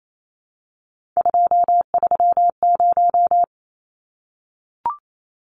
Morse code, heard as a keyed tone, about 28 wpm, 700 Hz. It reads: 230
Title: Numbers - 3-Digit 28wpm